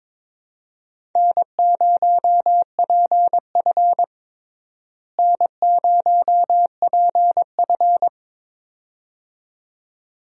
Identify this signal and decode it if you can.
Morse code, heard as a keyed tone, about 21 wpm, 700 Hz. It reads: N0PF N0PF
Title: USA Call Signs - 4 Characters - Repeated 2x - 22wpm